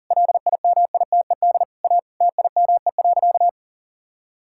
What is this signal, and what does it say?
Morse code, heard as a keyed tone, about 40 wpm, 700 Hz. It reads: LIMITED A TIME.
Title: Sentences from Top 1000 Words - Review Part 6 - 40wpm